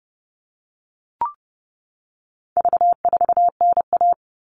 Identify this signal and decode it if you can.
Morse code, heard as a keyed tone, about 30 wpm, 700 Hz. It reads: V4NA
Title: All Cut Numbers - 4-Digit 30wpm